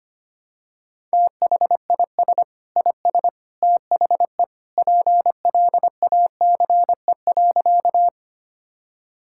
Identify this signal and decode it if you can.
Morse code, heard as a keyed tone, about 25 wpm, 700 Hz. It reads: THIS IS THE PLACE.
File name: practice / 25wpm / Sentences with 3-Word Phrases - 25wpm